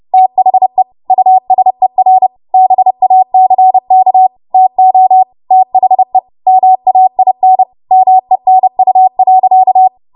Morse code, heard as a keyed tone, about 30 wpm, 750 Hz. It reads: THE USER BACK TO THE MAIN MENU.
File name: Morse / 30WPM